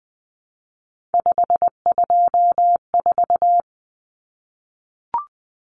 Morse code, heard as a keyed tone, about 20 wpm, 700 Hz. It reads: 524
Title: Common Cut Numbers - 3-Digit 20wpm